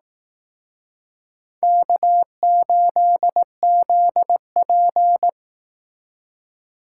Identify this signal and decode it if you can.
Morse code, heard as a keyed tone, about 18 wpm, 700 Hz. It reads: K8ZP
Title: Speed Racing - USA 1x2 Call Signs 15wpm